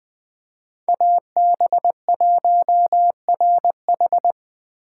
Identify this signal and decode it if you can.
Morse code, heard as a keyed tone, about 20 wpm, 700 Hz. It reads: AB1RH
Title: USA Call Signs - 5 Characters - 3x Character Spacing - 20wpm